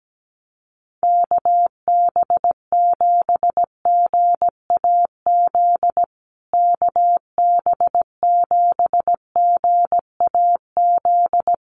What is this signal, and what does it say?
Morse code, heard as a keyed tone, about 17 wpm, 700 Hz. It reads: KB7GAZ KB7GAZ
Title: USA Call Signs Repeated 2x 17wpm